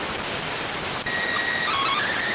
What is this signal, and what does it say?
SSTV (slow-scan television), PD 290.